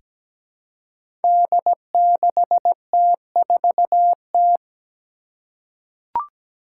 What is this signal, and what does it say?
Morse code, heard as a keyed tone, about 17 wpm, 700 Hz. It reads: D6T4T
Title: Five Challenging Letter-Number Combinations 17wpm